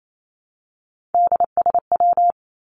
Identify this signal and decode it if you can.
Morse code, heard as a keyed tone, about 28 wpm, 700 Hz. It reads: DSW
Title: Abbreviations 28wpm